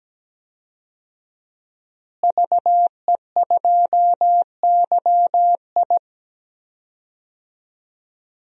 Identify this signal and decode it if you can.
Morse code, heard as a keyed tone, about 17 wpm, 700 Hz. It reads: VE2YI
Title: DXCC-100 Call Signs - 5 Characters - Repeated 3x - 17wpm